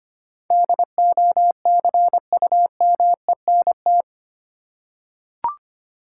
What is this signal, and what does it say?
Morse code, heard as a keyed tone, about 25 wpm, 690 Hz. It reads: DOCUMENT
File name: 8 Letter Words - First 3 Letters Given - 25wpm